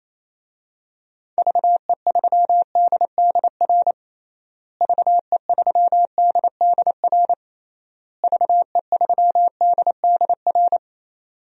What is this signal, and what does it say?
Morse code, heard as a keyed tone, about 28 wpm, 700 Hz. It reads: VE3DDR VE3DDR VE3DDR
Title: DXCC-100 Call Signs - 6 Characters - Repeated 3x - 28wpm